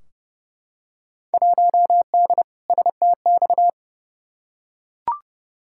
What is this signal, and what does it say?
Morse code, heard as a keyed tone, about 31 wpm, 700 Hz. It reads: 1D STX
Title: Field Day Exchanges - 30wpm